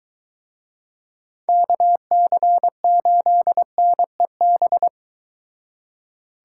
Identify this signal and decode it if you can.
Morse code, heard as a keyed tone, about 24 wpm, 700 Hz. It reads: KC8NEB